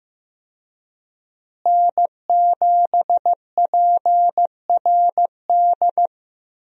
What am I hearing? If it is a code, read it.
Morse code, heard as a keyed tone, about 15 wpm, 700 Hz. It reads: N7PRD
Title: USA Call Signs - 5 Characters - Repeated 2x - 15wpm